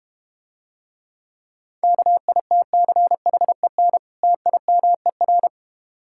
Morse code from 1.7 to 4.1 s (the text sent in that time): KITCHEN